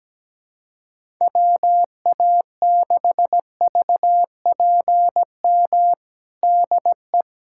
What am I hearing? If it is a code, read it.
Morse code, heard as a keyed tone, about 17 wpm, 700 Hz. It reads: WA6VPM DE
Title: USA Callsign DE Callsign - 6 Characters - 17wpm